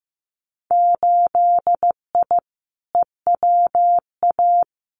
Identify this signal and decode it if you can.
Morse code, heard as a keyed tone, about 15 wpm, 700 Hz. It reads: 8I EWA
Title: Winter Field Day Exchanges Encore 15wpm